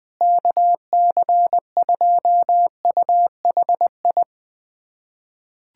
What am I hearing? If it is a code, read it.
Morse code, heard as a keyed tone, about 19 wpm, 700 Hz. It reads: KC2UHI
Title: USA Callsign DE Callsign 20wpm